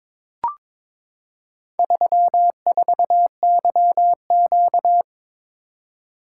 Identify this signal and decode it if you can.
Morse code, heard as a keyed tone, about 22 wpm, 700 Hz. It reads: 34YQ
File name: practice / 22wpm / Four Letter-Number Combinations - 22wpm